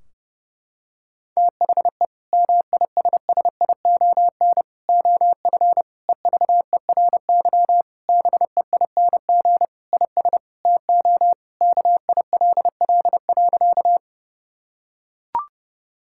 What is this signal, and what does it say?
Morse code, heard as a keyed tone, about 30 wpm, 700 Hz. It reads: THE MISSION OF EVERY BEING IS TO KILL.
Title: Sentences from Top 1600 Words 30wpm